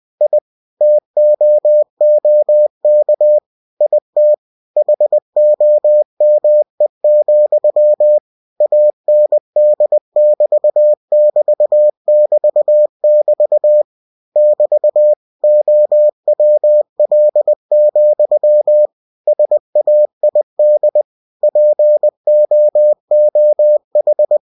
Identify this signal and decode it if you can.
Morse code, heard as a keyed tone, about 20 wpm, 600 Hz. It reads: I TOOK IT HOME, AND <BT> <BT> <BT> <BT> <BT> OWL, SAID POOH
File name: cw-books / winnie-the-pooh / winnie-20wpm-page-0032